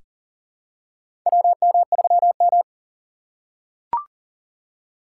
Morse code, heard as a keyed tone, about 40 wpm, 700 Hz. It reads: WM3M
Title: USA Call Signs - 4 Characters - 2x Character Spacing - 40wpm